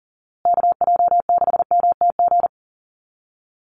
Morse code, heard as a keyed tone, about 39 wpm, 700 Hz. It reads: KJ6MTG
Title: USA Call Signs Repeated 3x 40wpm